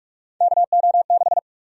Morse code, heard as a keyed tone, about 46 wpm, 700 Hz. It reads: KO6
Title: USA Call Sign Prefixes - 45wpm